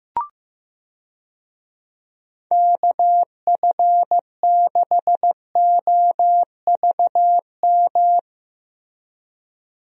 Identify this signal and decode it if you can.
Morse code, heard as a keyed tone, about 15 wpm, 700 Hz. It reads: KF6OVM
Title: USA Call Signs - 6 Characters - Repeated 2x - 15wpm